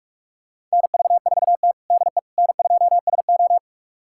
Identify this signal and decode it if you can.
Morse code, heard as a keyed tone, about 41 wpm, 700 Hz. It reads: NV4T DE N2SO
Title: USA Callsign DE Callsign - 4 Characters - 45wpm